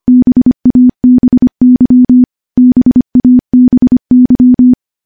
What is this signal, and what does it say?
Morse code, heard as a keyed tone, about 25 wpm, 250 Hz. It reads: BABY BABY
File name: morse-training-25WPM